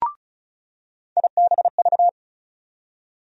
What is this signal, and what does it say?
Morse code, heard as a keyed tone, about 36 wpm, 700 Hz. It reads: IBV
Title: USA Call Sign Postfixes - 3 Letters 35wpm